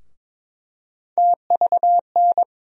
Morse code, heard as a keyed tone, about 22 wpm, 700 Hz. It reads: TVN
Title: USA Call Sign Postfixes - 3 Letters 22wpm